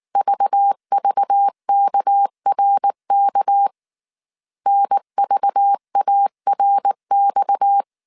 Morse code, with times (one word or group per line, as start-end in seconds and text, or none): 0.2-3.7 VVKRK
4.7-7.8 NVARX